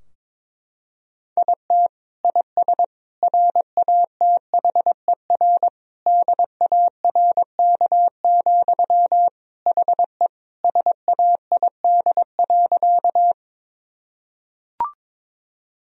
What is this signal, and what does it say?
Morse code, heard as a keyed tone, about 22 wpm, 700 Hz. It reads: IT IS RATHER DARK, HE SAID.
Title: Sentences from Top 1000 Words - Review Part 4 - 22wpm